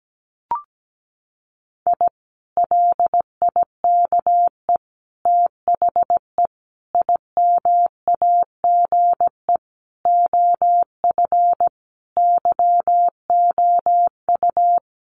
Morse code, heard as a keyed tone, about 17 wpm, 700 Hz. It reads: I LIKE THE IMAGE OF YOU
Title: Sentences from Top 700 Words 17wpm